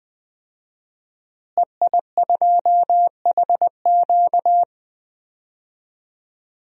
Morse code, heard as a keyed tone, about 20 wpm, 700 Hz. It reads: EI2HQ
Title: DXCC-100 Call Signs - 5 Characters - Repeated 3x - 20wpm